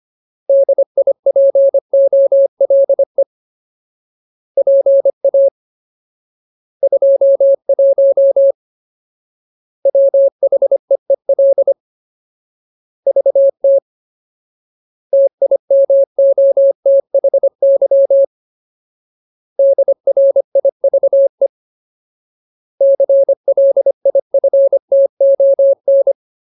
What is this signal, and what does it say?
Morse code, heard as a keyed tone, about 25 wpm, 550 Hz. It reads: DIPOLE PA 21 WHEEL VT TIMOTHY DRIVE CLIFTON
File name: Ultimate Word List / 25WPM Ultimate